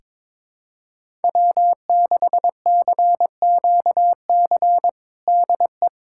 Morse code, heard as a keyed tone, about 22 wpm, 700 Hz. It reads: W6CQC DE
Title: USA Callsign DE Callsign 22wpm